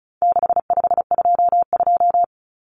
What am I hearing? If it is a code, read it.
Morse code, heard as a keyed tone, about 34 wpm, 700 Hz. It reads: 6522
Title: Common Cut Numbers - 4-Digit 35wpm